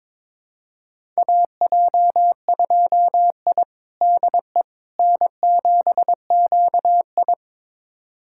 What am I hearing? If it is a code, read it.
Morse code, heard as a keyed tone, about 22 wpm, 700 Hz. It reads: AJ2I DE N7QI
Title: USA Callsign DE Callsign - 4 Characters - 22wpm